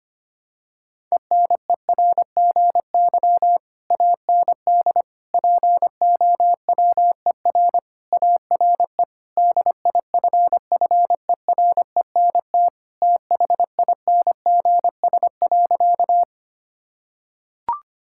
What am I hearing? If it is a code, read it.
Morse code, heard as a keyed tone, about 25 wpm, 700 Hz. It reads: ENERGY AND POWER ARE DIFFERENT THINGS.
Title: Sentences from Top 1000 Words - Review Part 6 - 25wpm